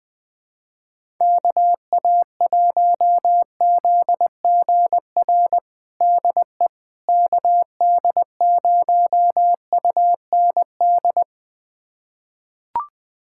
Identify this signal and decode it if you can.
Morse code, heard as a keyed tone, about 20 wpm, 700 Hz. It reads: KA1ZGR DE KD0UND
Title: USA Callsign DE Callsign - 6 Characters - 20wpm